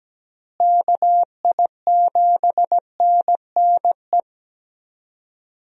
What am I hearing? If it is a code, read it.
Morse code, heard as a keyed tone, about 17 wpm, 700 Hz. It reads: KI7NNE
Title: USA Call Signs - 6 Characters - Repeated 2x - 17wpm